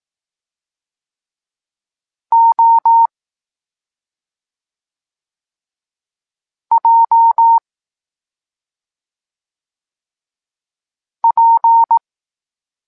Morse code, heard as a keyed tone, about 18 wpm, 920 Hz. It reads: O J P
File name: Tekst-26 3